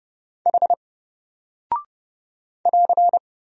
Morse code, heard as a keyed tone, about 30 wpm, 700 Hz. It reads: H <AR>
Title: Single Character (Letter:Number:Punctuation:Prosign) 30wpm